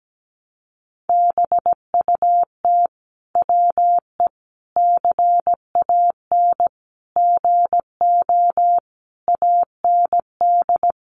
Morse code, heard as a keyed tone, about 17 wpm, 700 Hz. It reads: BUT WE CAN GO AND
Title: Sentences from Top 100 Words 17wpm